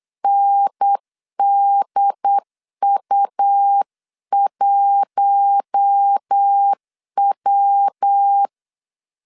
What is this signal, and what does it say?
Morse code, heard as a keyed tone, about 8 wpm, 800 Hz. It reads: NDU1W
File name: 8wpm